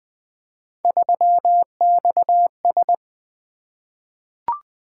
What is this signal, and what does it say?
Morse code, heard as a keyed tone, about 20 wpm, 700 Hz. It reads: 3XS
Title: Three Letter-Number Combinations - 20wpm